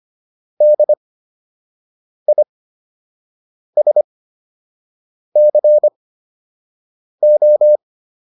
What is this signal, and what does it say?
Morse code, heard as a keyed tone, about 25 wpm, spaced as for 6 wpm, 600 Hz. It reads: DISCO